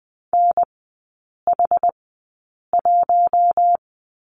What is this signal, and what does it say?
Morse code, heard as a keyed tone, about 20 wpm, spaced as for 8 wpm, 690 Hz. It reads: NH1
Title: 060 Lesson 11e - Letter h - Three Characters 20wpm